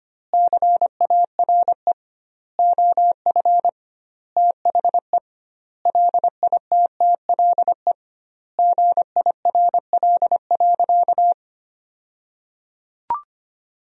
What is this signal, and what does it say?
Morse code, heard as a keyed tone, about 25 wpm, 700 Hz. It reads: CARE OF THE LITTLE GIRL.
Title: 2x Word-Spacing - Sentences from Top 400 Words - 25wpm